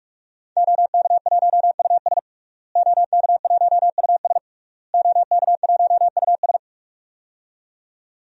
Morse code, heard as a keyed tone, about 41 wpm, 700 Hz. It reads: OK1US OK1US OK1US
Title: DXCC-100 Call Signs - 5 Characters - Repeated 3x - 45wpm